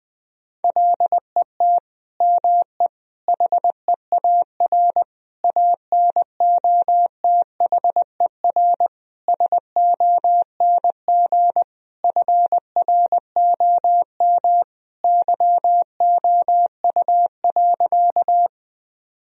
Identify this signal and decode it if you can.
Morse code, heard as a keyed tone, about 20 wpm, 700 Hz. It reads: LET ME HEAR ANOTHER SONG FROM YOU.
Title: Sentences from Top 1000 Words - Review Part 2 - 20wpm